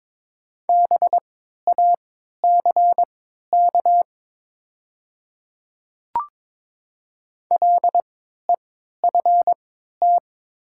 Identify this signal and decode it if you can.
Morse code, heard as a keyed tone, about 22 wpm, spaced as for 13 wpm, 700 Hz. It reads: BACK LEFT
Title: 3x Character Spacing - Top 300 Words - 22wpm